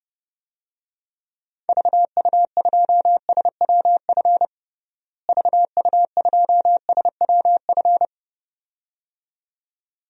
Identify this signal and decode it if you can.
Morse code, heard as a keyed tone, about 29 wpm, 700 Hz. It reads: VU2SWF VU2SWF
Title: DXCC-100 Call Signs - 6 Characters - Repeated 2x - 30wpm